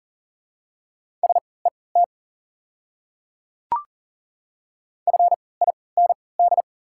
Morse code, heard as a keyed tone, about 40 wpm, spaced as for 23 wpm, 700 Hz. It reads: SET FIND
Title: 3x Character Spacing - Top 200 Words - 40wpm